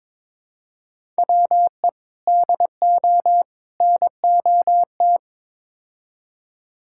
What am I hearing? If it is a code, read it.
Morse code, heard as a keyed tone, about 22 wpm, 700 Hz. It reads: WE DO NOT
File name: practice / 22wpm / Build A Phrase - 5-Word Phrases - 22wpm